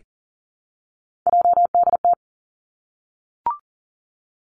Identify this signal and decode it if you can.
Morse code, heard as a keyed tone, about 42 wpm, 700 Hz. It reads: JDT
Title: Three Challenging Letter-Number Combinations 40wpm